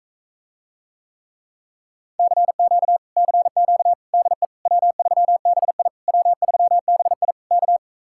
Morse code, heard as a keyed tone, about 45 wpm, 700 Hz. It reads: CQ CQ DE W3BI W3BI K